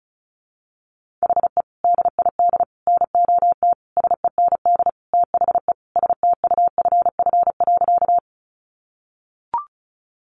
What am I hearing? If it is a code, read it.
Morse code, heard as a keyed tone, about 34 wpm, 700 Hz. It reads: HE DID NOT SEND THE STUFF.
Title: Sentences from Top 1500 Words 35wpm